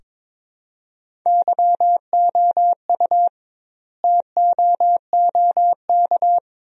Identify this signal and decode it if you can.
Morse code, heard as a keyed tone, about 22 wpm, 700 Hz. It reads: YOU TOOK
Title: 2x Word-Spacing - Sentences from Top 300 Words - 22wpm